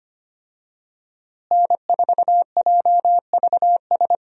Morse code, heard as a keyed tone, about 25 wpm, 700 Hz. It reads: N4JVS
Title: USA Call Signs - 5 Characters - Repeated 3x - 25wpm